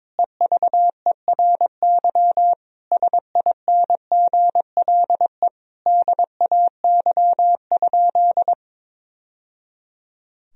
Morse code, heard as a keyed tone, about 22 wpm, 700 Hz. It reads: EVERY SINGLE DAY?
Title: Sentences from Top 1000 Words - Review Part 4 - 22wpm